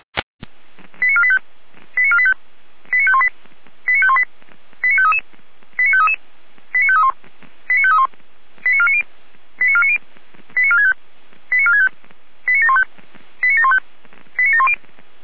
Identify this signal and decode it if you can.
Five-tone selective call (ZVEI), heard as quick runs of five tones, tones of about 70 ms. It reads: 89485 89485 89518 89518 89533 89533 89531 89531 89490 89490 89465 89465 89715 89715 89710
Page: ZVEI